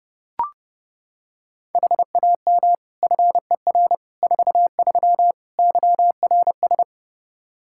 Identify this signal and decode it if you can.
Morse code, heard as a keyed tone, about 32 wpm, 700 Hz. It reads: HAM FER 43 YRS
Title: Practice QSO - Exchange 3 - More Info - 30wpm